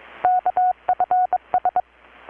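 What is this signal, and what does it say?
Morse code, heard as a keyed tone, about 22 wpm, 720 Hz. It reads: KFS
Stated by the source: Morse from KPH's Night of Nights XX maritime radiotelegraphy event, received in Chiapas, Mexico about 3,800 km away on a Kenwood R-600 with a grounded 10 m long-wire antenna.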